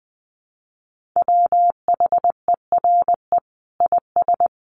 Morse code, heard as a keyed tone, about 20 wpm, 700 Hz. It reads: WHERE IS
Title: Sentences from Top 1500 Words 20wpm